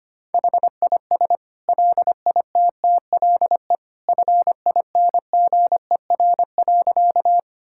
Morse code, heard as a keyed tone, about 25 wpm, 700 Hz. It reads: HIS LITTLE FINGER.
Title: Sentences from Top 1600 Words 25wpm